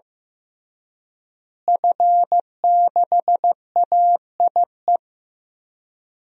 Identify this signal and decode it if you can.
Morse code, heard as a keyed tone, about 15 wpm, 700 Hz. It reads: F6AIE
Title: DXCC-100 Call Signs - 5 Characters - Repeated 3x - 15wpm